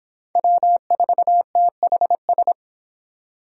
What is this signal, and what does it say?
Morse code, heard as a keyed tone, about 31 wpm, 700 Hz. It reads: W4THS